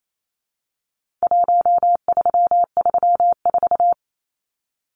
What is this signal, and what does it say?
Morse code, heard as a keyed tone, about 28 wpm, 700 Hz. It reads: 1334
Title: Numbers - Contest Style - 4-Digit 28wpm